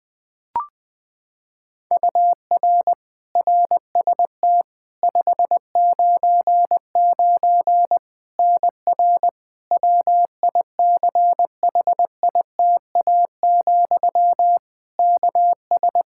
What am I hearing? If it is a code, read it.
Morse code, heard as a keyed tone, about 20 wpm, 700 Hz. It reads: UR RST 599 NR WICHITA, KS
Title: K1USN SST QSO Practice 20wpm